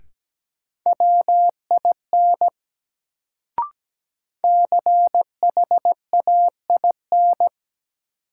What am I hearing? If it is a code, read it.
Morse code, heard as a keyed tone, about 17 wpm, 700 Hz. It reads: WIN CHAIN
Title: Sentences from Top 1500 Words - New Word Drill Alt - 17wpm